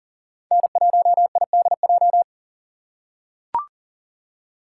Morse code, heard as a keyed tone, about 39 wpm, 700 Hz. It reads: N1IDJ
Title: USA Call Signs - 5 Characters - 2x Character Spacing - 40wpm